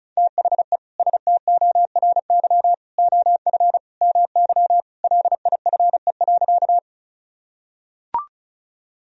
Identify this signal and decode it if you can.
Morse code, heard as a keyed tone, about 36 wpm, 700 Hz. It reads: THE STORY OF MY LIFE.
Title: Sentences from Top 1000 Words - Review Part 1 - 35wpm